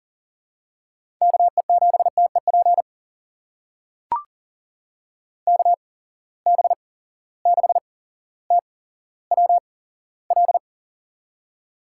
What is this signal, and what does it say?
Morse code, heard as a keyed tone, about 43 wpm, spaced as for 17 wpm, 700 Hz. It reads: KE7TEP KB6TWL
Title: USA Call Signs - 6 Characters - 8x Character Spacing - 40wpm